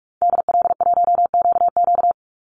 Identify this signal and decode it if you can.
Morse code, heard as a keyed tone, about 44 wpm, 700 Hz. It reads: DL1QQ
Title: Call Signs - 45wpm